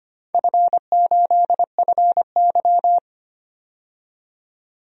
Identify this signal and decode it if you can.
Morse code, heard as a keyed tone, about 25 wpm, 700 Hz. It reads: F8FY
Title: DXCC-100 Call Signs - 4 Characters - 25wpm